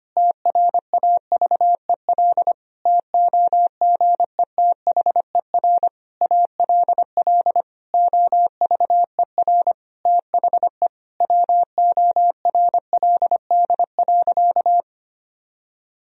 Morse code, heard as a keyed tone, about 25 wpm, 700 Hz. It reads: TRAVEL TOGETHER ALL OVER THE WORLD.
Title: Sentences from Top 1000 Words - Review Part 2 - 25wpm